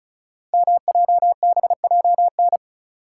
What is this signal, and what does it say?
Morse code, heard as a keyed tone, about 36 wpm, 690 Hz. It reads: MJBJN
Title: Code Groups - 5 Letters - 35wpm